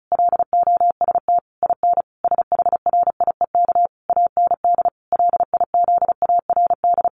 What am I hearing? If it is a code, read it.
Morse code, heard as a keyed tone, about 34 wpm, 700 Hz. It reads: LOST IN SHRIEK AND LIZARD
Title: A Sound Of Thunder 35wpm